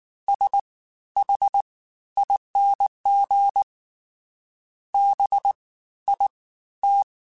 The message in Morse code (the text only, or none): S H ING B I T